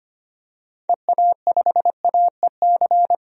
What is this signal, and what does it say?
Morse code, heard as a keyed tone, about 25 wpm, 700 Hz. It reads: EA5AEC
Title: DXCC-100 Call Signs - 6 Characters - 25wpm